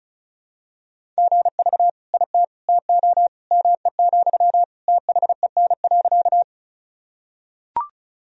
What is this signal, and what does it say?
Morse code, heard as a keyed tone, about 34 wpm, 700 Hz. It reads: GV IT TO ME, THEN.
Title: Sentences from Top 200 Words and Abbreviations 35wpm